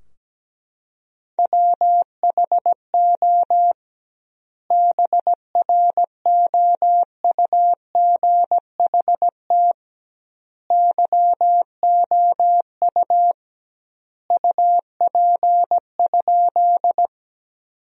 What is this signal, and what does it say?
Morse code, heard as a keyed tone, about 17 wpm, 700 Hz. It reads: WHO BROUGHT YOU UP?
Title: 2x Word-Spacing - Sentences from Top 500 Words - 17wpm